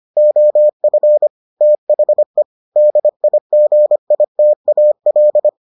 Morse code, heard as a keyed tone, about 25 wpm, 600 Hz. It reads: OF THE DIGITAL